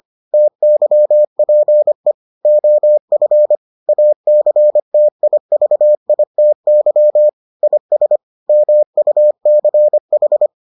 Morse code, heard as a keyed tone, about 25 wpm, 600 Hz. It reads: TYPE OF ACTIVITY IS MUCH